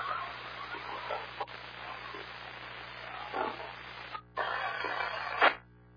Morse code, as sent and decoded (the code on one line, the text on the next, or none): none